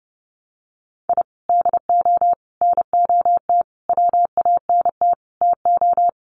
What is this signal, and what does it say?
Morse code, heard as a keyed tone, about 30 wpm, 700 Hz. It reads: I DO NOT WANT TO
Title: Sets of 5 Words Encore2 30wpm